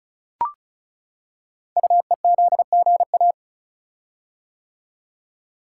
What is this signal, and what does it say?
Morse code, heard as a keyed tone, about 36 wpm, 690 Hz. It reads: UEZGA
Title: Code Groups - 5 Letter:Numbers - 35wpm